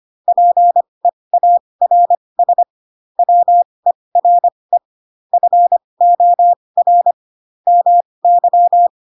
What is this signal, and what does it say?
Morse code, heard as a keyed tone, about 25 wpm, spaced as for 20 wpm, 700 Hz. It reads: PEARS WERE FOR MY
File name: Pinocchio / Pinocchio ch7 15 25 ik1hge